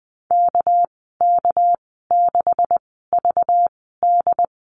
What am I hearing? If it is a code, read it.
Morse code, heard as a keyed tone, about 19 wpm, spaced as for 14 wpm, 700 Hz. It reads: KK6VD
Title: 231 Lesson 37g - Letter x - Call Signs 20wpm